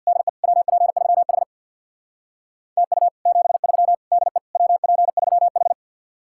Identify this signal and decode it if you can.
Morse code, heard as a keyed tone, about 66 wpm, 700 Hz. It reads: DE WP3H TU 73 DE WP3H